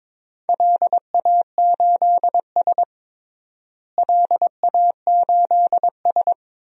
Morse code, heard as a keyed tone, about 22 wpm, 700 Hz. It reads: LA8S LA8S
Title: DXCC-100 Call Signs - 4 Characters - Repeated 2x - 22wpm